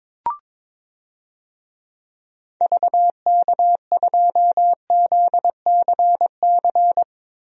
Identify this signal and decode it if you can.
Morse code, heard as a keyed tone, about 21 wpm, 700 Hz. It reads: VK2ZCC
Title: DXCC-100 Call Signs - 6 Characters - Repeated 2x - 22wpm